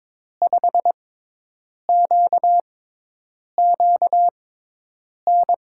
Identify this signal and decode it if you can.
Morse code, heard as a keyed tone, about 22 wpm, spaced as for 8 wpm, 700 Hz. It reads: 5QQN
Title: USA Call Signs - 6 Characters - 6x Character Spacing - 22wpm